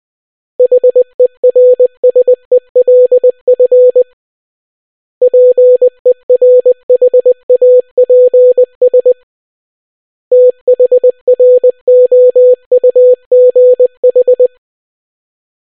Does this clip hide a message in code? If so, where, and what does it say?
Yes: Morse code from 0.6 to 14.5 s, heard as a keyed tone, about 20 wpm, 500 Hz. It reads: HERSELF PERHAPS THROUGH